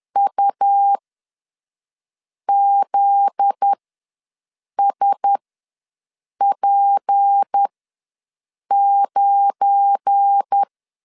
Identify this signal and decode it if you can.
Morse code, heard as a keyed tone, about 10 wpm, spaced as for 5 wpm, 800 Hz. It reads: UZSP9
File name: LEZIONI / 7 GF9Z / 7 GF9Z ws 10wpm 3919